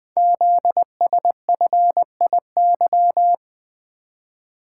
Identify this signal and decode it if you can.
Morse code, heard as a keyed tone, about 20 wpm, 700 Hz. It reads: ZSFIY
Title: Code Groups - 5 Letters - 20wpm